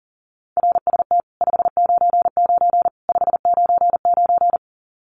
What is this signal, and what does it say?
Morse code, heard as a keyed tone, about 40 wpm, 700 Hz. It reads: RST 599 599
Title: Common QSO Phrases 40wpm